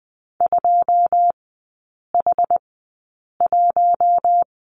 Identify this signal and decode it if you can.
Morse code, heard as a keyed tone, about 20 wpm, spaced as for 8 wpm, 700 Hz. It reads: 2H1
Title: 081 Lesson 14e - Number 2 - Three Characters 20wpm